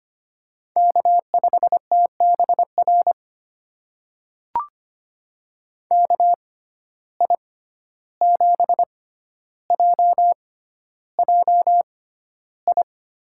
Morse code, heard as a keyed tone, about 25 wpm, spaced as for 12 wpm, 700 Hz. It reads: K5TBR KI7JJI
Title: USA Call Signs - 6x Character Spacing - 25wpm